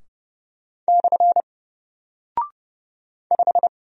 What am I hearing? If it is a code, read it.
Morse code, heard as a keyed tone, about 30 wpm, 700 Hz. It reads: / 5
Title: Single Character (Letter:Number:Punctuation:Prosign) Alt 30wpm